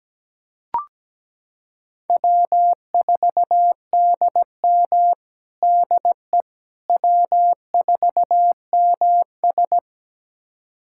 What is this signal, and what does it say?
Morse code, heard as a keyed tone, about 17 wpm, 700 Hz. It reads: W4DM DE W4MS
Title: USA Callsign DE Callsign - 4 Characters - 17wpm